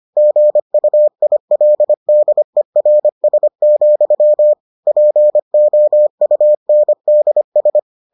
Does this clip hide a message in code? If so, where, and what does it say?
Yes: Morse code from 0.2 to 7.8 s, heard as a keyed tone, about 25 wpm, 600 Hz. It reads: GUILDERS, POUNDS